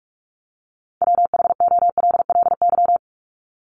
Morse code, heard as a keyed tone, about 48 wpm, 700 Hz. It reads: WHOLLY
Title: LY Suffix - Top 100 Words with Base Words 45wpm